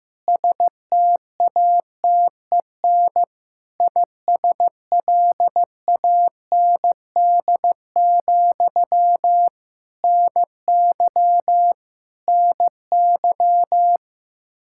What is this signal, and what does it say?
Morse code, heard as a keyed tone, about 15 wpm, 700 Hz. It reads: STATEN ISLAND, NY NY
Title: Practice QSO - Exchange 1 - Beginning - 15wpm